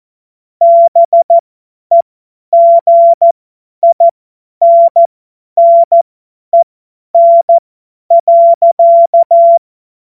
Morse code, heard as a keyed tone, about 14 wpm, spaced as for 10 wpm, 680 Hz. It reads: BEGINNEN.
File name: Witz-19-August-2025-50-F70